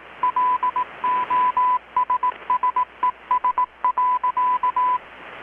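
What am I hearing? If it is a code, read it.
Morse code, heard as a keyed tone, about 18 wpm, 1000 Hz, with radio noise. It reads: LOSSES.
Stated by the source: Morse from KPH's Night of Nights XX maritime radiotelegraphy event, received in Chiapas, Mexico about 3,800 km away on a Kenwood R-600 with a grounded 10 m long-wire antenna.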